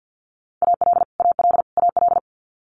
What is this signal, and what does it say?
Morse code, heard as a keyed tone, about 50 wpm, 700 Hz. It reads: AL AL AL
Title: Multi-letter Phonemes Repeated 3x 50wpm